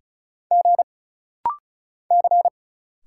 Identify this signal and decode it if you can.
Morse code, heard as a keyed tone, about 36 wpm, 700 Hz. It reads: G C
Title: Single Letters - Rapid-Fire 35wpm